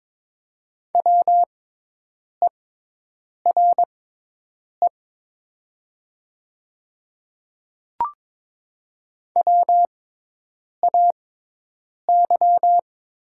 Morse code, heard as a keyed tone, about 22 wpm, spaced as for 8 wpm, 700 Hz. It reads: WERE WAY
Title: 6x Character Spacing - Top 100 Words - 22wpm